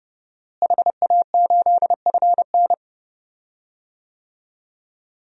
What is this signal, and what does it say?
Morse code, heard as a keyed tone, about 30 wpm, 700 Hz. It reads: HA8FN
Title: DXCC-100 Call Signs - 5 Characters - Repeated 3x - 30wpm